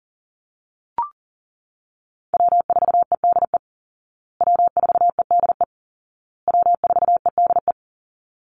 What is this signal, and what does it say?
Morse code, heard as a keyed tone, about 40 wpm, 700 Hz. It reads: W4EDE W4EDE W4EDE
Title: Call Signs Repeated 2x - 40wpm